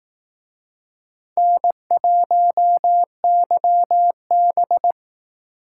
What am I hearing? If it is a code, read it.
Morse code, heard as a keyed tone, about 18 wpm, 700 Hz. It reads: N1YB
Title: Speed Racing - USA Call Signs 15wpm